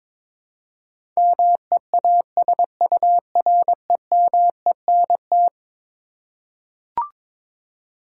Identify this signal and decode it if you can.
Morse code, heard as a keyed tone, about 22 wpm, 700 Hz. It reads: MEASUREMENT
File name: practice / 22wpm / Eleven Letter Words - 22wpm